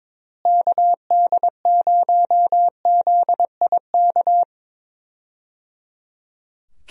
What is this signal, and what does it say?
Morse code, heard as a keyed tone, about 22 wpm, 700 Hz. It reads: KD0ZIK
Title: USA Call Signs - 6 Characters - Repeated 3x - 22wpm